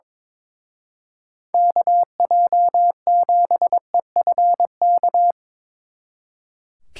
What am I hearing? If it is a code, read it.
Morse code, heard as a keyed tone, about 24 wpm, 700 Hz. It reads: KJ7EFK
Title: Speed Racing - USA 2x3 Call Signs 22wpm